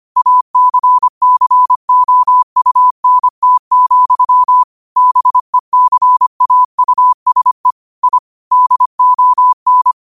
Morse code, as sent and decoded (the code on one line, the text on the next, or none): .- -.-. -.-. --- ..- -. - --..-- / -... -.-. .- ..- ... / .. / -.. --- -.
ACCOUNT, BECAUSE I DON